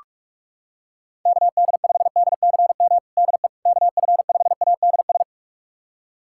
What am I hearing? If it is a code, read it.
Morse code, heard as a keyed tone, about 45 wpm, 700 Hz. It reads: KD5DCM DE KF5ADS
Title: USA Callsign DE Callsign 45wpm